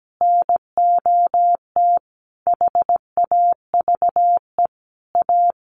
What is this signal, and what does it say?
Morse code, heard as a keyed tone, about 17 wpm, 700 Hz. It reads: NOT HAVE A
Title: Sets of 5 Words 17wpm